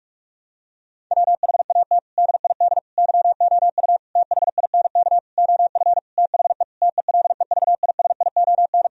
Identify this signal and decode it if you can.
Morse code, heard as a keyed tone, about 46 wpm, 700 Hz. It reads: WHAT DID YOU THINK OF THE TELEVISION
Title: Sentences from Top 1000 Words - Review Part 4 - 45wpm